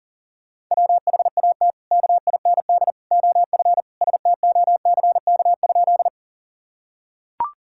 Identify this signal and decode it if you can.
Morse code, heard as a keyed tone, about 37 wpm, 700 Hz. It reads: WHAT KIND OF STOCK?
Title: Sentences from Top 1000 Words - Review Part 5 - 40wpm